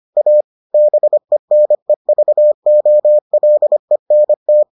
Morse code, heard as a keyed tone, about 25 wpm, 600 Hz. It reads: A BENEVOLENT